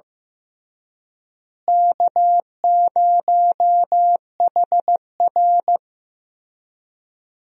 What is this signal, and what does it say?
Morse code, heard as a keyed tone, about 15 wpm, 700 Hz. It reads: K0HR
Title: USA Call Signs - 4 Characters - Repeated 3x - 15wpm